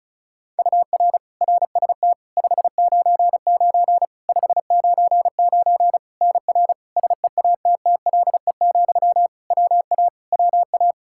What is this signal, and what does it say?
Morse code, heard as a keyed tone, about 34 wpm, 700 Hz. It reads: UR RST 599 599 NR SEATTLE, WA WA
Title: Practice QSO - Exchange 1 - Beginning - 35wpm